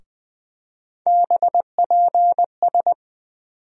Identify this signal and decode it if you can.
Morse code, heard as a keyed tone, about 20 wpm, 700 Hz. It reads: BPS
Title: USA Call Sign Postfixes - 3 Letters 20wpm